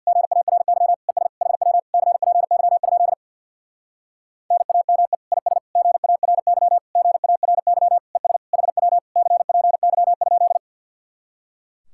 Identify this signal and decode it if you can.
Morse code, heard as a keyed tone, about 60 wpm, 700 Hz. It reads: GARY ES HW CPY? NAME IS GARY GARY ES HW CPY?
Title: Speed-Racing Common QSO Phrases - 50wpm